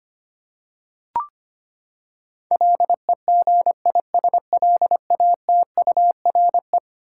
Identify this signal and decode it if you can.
Morse code, heard as a keyed tone, about 25 wpm, 700 Hz. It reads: LEGISLATURE
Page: morse-code-ninja-practice-25wpm